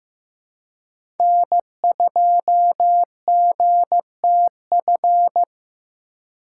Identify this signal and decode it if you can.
Morse code, heard as a keyed tone, about 15 wpm, 700 Hz. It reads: N2GTF
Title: USA Call Signs - 5 Characters - 4x Character Spacing - 15wpm